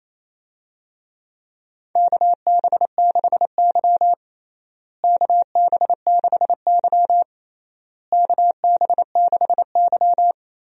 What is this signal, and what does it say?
Morse code, heard as a keyed tone, about 28 wpm, 700 Hz. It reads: KB6Y KB6Y KB6Y
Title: USA Call Signs - 4 Characters - Repeated 3x - 28wpm